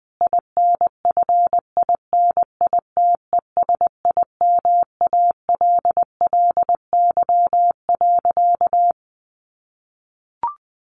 Morse code, heard as a keyed tone, about 20 wpm, 700 Hz. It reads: INFINITESIMALLY.
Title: A Sound Of Thunder 20wpm